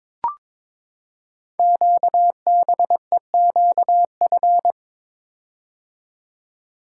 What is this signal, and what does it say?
Morse code, heard as a keyed tone, about 23 wpm, 700 Hz. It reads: QBEQF
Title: Code Groups - 5 Letters - 22wpm